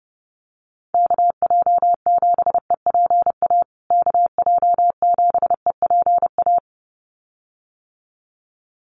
Morse code, heard as a keyed tone, about 30 wpm, 700 Hz. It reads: KJ7EPA KJ7EPA
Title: USA Call Signs Repeated 2x 30wpm